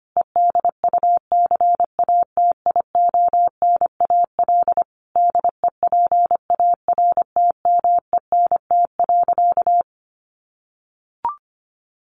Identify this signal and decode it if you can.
Morse code, heard as a keyed tone, about 25 wpm, 700 Hz. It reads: EDUCATIONAL DEPARTMENT.
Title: Sentences from Top 1200 Words 25wpm